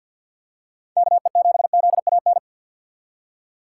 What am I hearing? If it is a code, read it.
Morse code, heard as a keyed tone, about 53 wpm, 700 Hz. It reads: KE7ZAN
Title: USA Call Signs - 6x Character Spacing - 50wpm